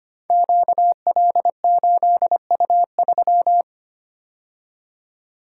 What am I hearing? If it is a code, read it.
Morse code, heard as a keyed tone, about 25 wpm, 700 Hz. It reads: QL8U3
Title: Code Groups - 5 Letter:Numbers - 25wpm